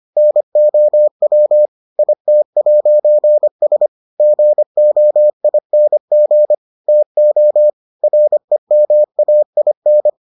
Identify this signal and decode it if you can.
Morse code, heard as a keyed tone, about 25 wpm, 600 Hz. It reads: NOW IT'S GOING TO REMAIN